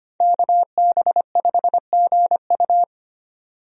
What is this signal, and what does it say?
Morse code, heard as a keyed tone, about 25 wpm, 700 Hz. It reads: KB5GU
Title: USA Call Signs - 5 Characters - 25wpm